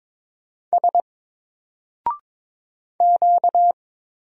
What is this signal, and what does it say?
Morse code, heard as a keyed tone, about 22 wpm, 700 Hz. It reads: S Q
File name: Single-Letter-Number-22wpm